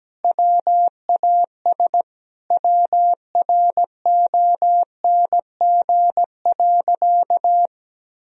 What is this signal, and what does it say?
Morse code, heard as a keyed tone, about 17 wpm, 700 Hz. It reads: WAS WRONG.
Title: Sentences from Top 1000 Words - Review Part 2 - 17wpm